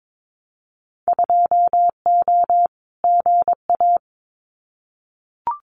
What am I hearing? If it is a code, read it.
Morse code, heard as a keyed tone, about 22 wpm, 700 Hz. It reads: 2O GA
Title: Winter Field Day Exchanges Encore 22wpm